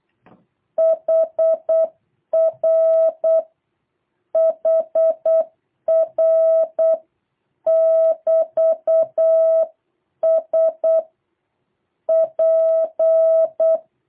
Morse code, heard as a keyed tone, about 8 wpm, 650 Hz. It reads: HR HR <BT> S P
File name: jh8cbh / 2023cw